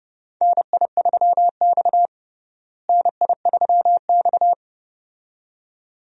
Morse code, heard as a keyed tone, about 31 wpm, 700 Hz. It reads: NI3X NI3X
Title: USA Call Signs - 4 Characters - Repeated 2x - 30wpm